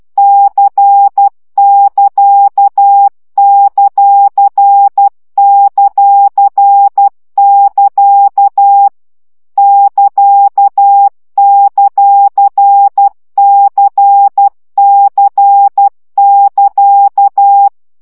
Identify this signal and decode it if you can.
Morse code, heard as a keyed tone, about 12 wpm, 800 Hz. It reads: C<KA>;;<KA> <KA>;CC<KA>
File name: CW gyakorlas c ct pontosvesszo 10 baud